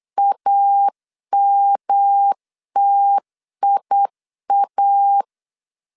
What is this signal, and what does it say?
Morse code, heard as a keyed tone, about 8 wpm, 800 Hz. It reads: AMTIA